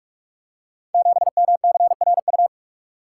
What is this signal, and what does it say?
Morse code, heard as a keyed tone, about 44 wpm, 690 Hz. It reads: 7MCRU
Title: Code Groups - 5 Letter:Numbers - 45wpm